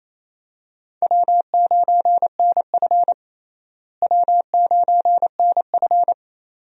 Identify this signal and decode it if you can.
Morse code, heard as a keyed tone, about 28 wpm, 700 Hz. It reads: W9NF W9NF
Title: USA Call Signs - 4 Characters - Repeated 2x - 28wpm